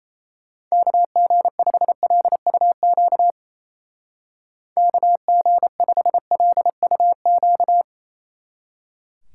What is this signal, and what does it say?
Morse code, heard as a keyed tone, about 35 wpm, 700 Hz. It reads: KG5LUQ KG5LUQ
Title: Speed Racing - USA 2x3 Call Signs 28wpm